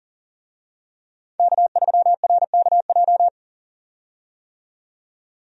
Morse code, heard as a keyed tone, about 40 wpm, 700 Hz. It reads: K3RKJ
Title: USA Call Signs - 5 Characters - 40wpm